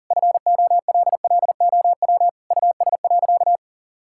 Morse code, heard as a keyed tone, about 40 wpm, 700 Hz. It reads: FOLLOW US.
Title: Sentences from Top 1000 Words - Review Part 6 - 40wpm